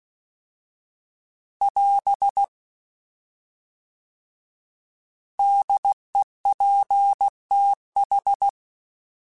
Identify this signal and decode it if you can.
Morse code, heard as a keyed tone, about 16 wpm, 800 Hz. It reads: <AS> DEPTH